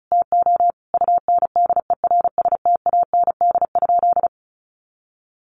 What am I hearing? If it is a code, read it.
Morse code, heard as a keyed tone, about 34 wpm, 700 Hz. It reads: TO UNDERSTAND?
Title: Sentences from Top 1100 Words 35wpm